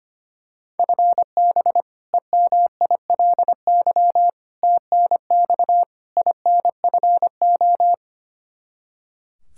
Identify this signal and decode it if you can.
Morse code, heard as a keyed tone, about 25 wpm, 700 Hz. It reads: FB EMILY TNX INFO
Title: Practice QSO - Exchange 3 - More Info - 25wpm